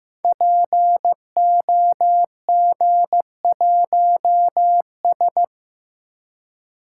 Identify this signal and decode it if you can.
Morse code, heard as a keyed tone, about 15 wpm, 700 Hz. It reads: POG1S
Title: Code Groups - 5 Letter:Numbers - 15wpm